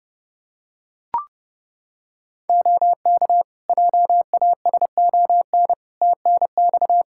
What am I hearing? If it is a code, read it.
Morse code, heard as a keyed tone, about 32 wpm, 700 Hz. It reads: OK JASON TNX
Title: Practice QSO - Exchange 4 - Ending - 30wpm